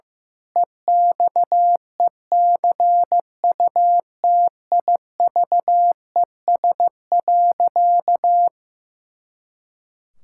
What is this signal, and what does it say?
Morse code, heard as a keyed tone, about 15 wpm, 700 Hz. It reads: EXECUTIVES.
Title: Pangram Challenge v2 - 15wpm